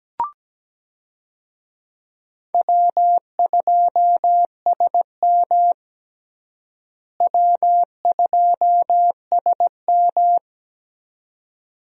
Morse code, heard as a keyed tone, about 17 wpm, 700 Hz. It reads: W2SM W2SM
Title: USA Call Signs - 4 Characters - Repeated 3x - 17wpm